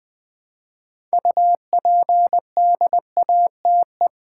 Morse code, heard as a keyed tone, about 20 wpm, 690 Hz. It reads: UPDATE
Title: Build A Word - 3 Letters At A Time - 6-Letter Words - 20wpm